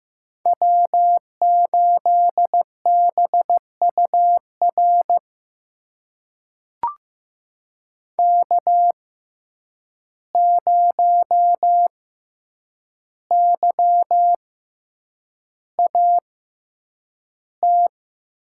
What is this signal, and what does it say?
Morse code, heard as a keyed tone, about 15 wpm, spaced as for 8 wpm, 700 Hz. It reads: W8BUR K0YAT
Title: USA Call Signs - 5 Characters - 6x Character Spacing - 15wpm